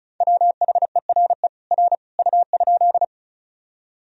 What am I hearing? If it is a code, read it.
Morse code, heard as a keyed tone, about 36 wpm, 700 Hz. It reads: WHERE R U?
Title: Sentences from Top 200 Words and Abbreviations 35wpm